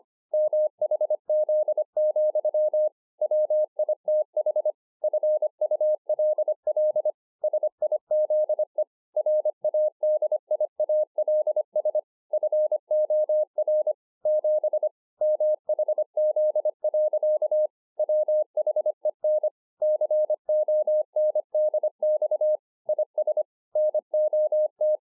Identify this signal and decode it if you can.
Morse code, heard as a keyed tone, about 25 wpm, 610 Hz. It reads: MHZ, WITH FULL SIZE RADIALS FOR 7 MHZ. WHEN CONDX IS NOT